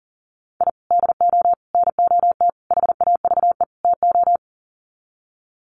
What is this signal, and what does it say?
Morse code, heard as a keyed tone, about 37 wpm, 700 Hz. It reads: I DO NOT HAVE TO
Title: Sets of 5 Words 40wpm